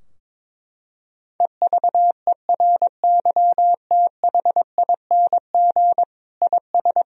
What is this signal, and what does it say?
Morse code, heard as a keyed tone, about 22 wpm, 700 Hz. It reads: EVERYTHING IS
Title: Sentences from Top 1000 Words - Review Part 2 - 22wpm